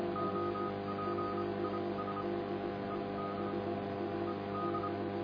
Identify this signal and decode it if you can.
Morse code, heard as a keyed tone, about 26 wpm, spaced as for 22 wpm, 1280 Hz, with noise. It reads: OPIS ED ED